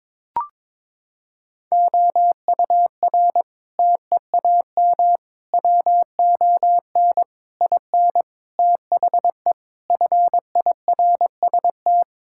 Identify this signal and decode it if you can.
Morse code, heard as a keyed tone, about 21 wpm, 700 Hz. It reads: OUR TEAM WON IN THE FIRST
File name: Sentences with 7-Word Phrases - 22wpm